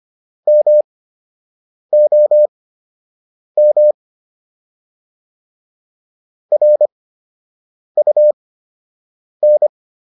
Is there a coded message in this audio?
Yes: Morse code from 0.5 to 9.7 s, heard as a keyed tone, about 25 wpm, spaced as for 7 wpm, 600 Hz. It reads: MOM RUN